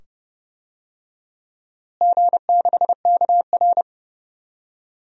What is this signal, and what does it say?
Morse code, heard as a keyed tone, about 30 wpm, 700 Hz. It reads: G6KR
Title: DXCC-100 Call Signs - 4 Characters - Repeated 2x - 30wpm